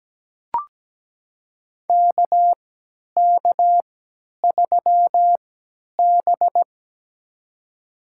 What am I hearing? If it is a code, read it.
Morse code, heard as a keyed tone, about 17 wpm, spaced as for 10 wpm, 700 Hz. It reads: KK3B
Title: USA Call Signs - 4 Characters - 3x Character Spacing - 17wpm